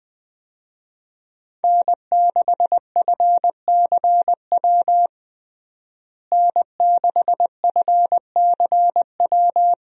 Morse code, heard as a keyed tone, about 21 wpm, 700 Hz. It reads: N6FCW N6FCW
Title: USA Call Signs - 5 Characters - Repeated 3x - 20wpm